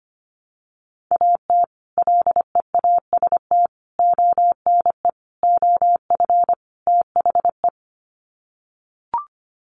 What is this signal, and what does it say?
Morse code, heard as a keyed tone, about 25 wpm, 700 Hz. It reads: AT LEAST ONE OF THE
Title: Sets of 5 Words Encore 25wpm